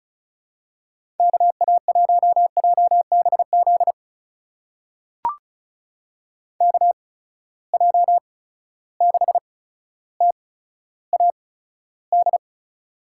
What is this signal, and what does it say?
Morse code, heard as a keyed tone, about 36 wpm, spaced as for 15 wpm, 700 Hz. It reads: KA1JBZ KJ6TAD
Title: USA Call Signs - 6 Characters - 8x Character Spacing - 35wpm